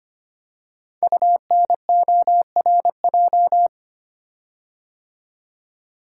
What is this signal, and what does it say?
Morse code, heard as a keyed tone, about 25 wpm, 700 Hz. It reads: UNORJ
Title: Code Groups - 5 Letters - 25wpm